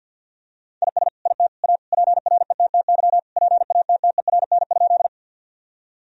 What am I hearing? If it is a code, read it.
Morse code, heard as a keyed tone, about 52 wpm, 700 Hz. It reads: IS IT A PRETTY PATTERN?
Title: Sentences from Top 1000 Words - Review Part 2 - 50wpm